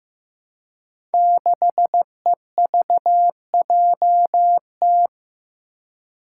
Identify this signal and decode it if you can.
Morse code, heard as a keyed tone, about 15 wpm, 700 Hz. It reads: 6EVJT
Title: Five Challenging Letter-Number Combinations 15wpm